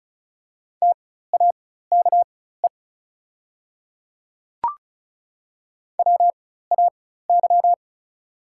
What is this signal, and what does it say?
Morse code, heard as a keyed tone, about 34 wpm, spaced as for 16 wpm, 690 Hz. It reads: TAKE WAY
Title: 4x Character Spacing - Top 200 Words - 35wpm